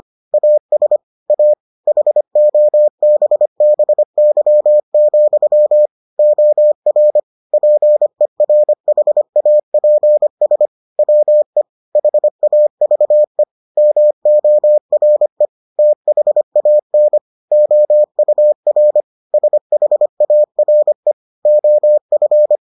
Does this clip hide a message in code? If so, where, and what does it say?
Yes: Morse code from 0.3 to 22.6 s, heard as a keyed tone, about 25 wpm, 600 Hz. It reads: AS A HOBBY, OR PERHAPS WE HAVE MORE THAN OUR SHARE OF